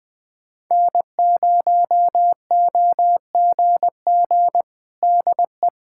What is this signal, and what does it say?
Morse code, heard as a keyed tone, about 21 wpm, 700 Hz. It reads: N0OGG DE
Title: USA Callsign DE Callsign 20wpm